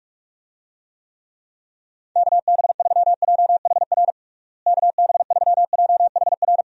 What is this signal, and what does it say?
Morse code, heard as a keyed tone, about 46 wpm, 700 Hz. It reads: KB3JHR KB3JHR
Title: USA Call Signs - 6 Characters - Repeated 2x - 45wpm